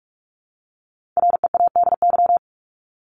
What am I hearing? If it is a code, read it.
Morse code, heard as a keyed tone, about 44 wpm, 700 Hz. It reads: READY
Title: Top 500 Words 45wpm